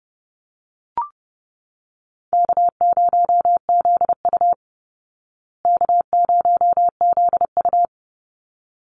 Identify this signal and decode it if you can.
Morse code, heard as a keyed tone, about 30 wpm, 700 Hz. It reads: K0ZU K0ZU
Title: Easy Call Signs Repeated 2x - 30wpm